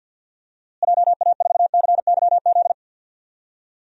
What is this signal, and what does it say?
Morse code, heard as a keyed tone, about 50 wpm, 700 Hz. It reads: JA4CYZ
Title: DXCC-100 Call Signs - 6 Characters - Repeated 3x - 50wpm